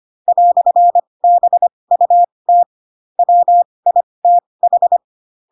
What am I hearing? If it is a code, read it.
Morse code, heard as a keyed tone, about 25 wpm, spaced as for 20 wpm, 700 Hz. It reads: "BUT WITH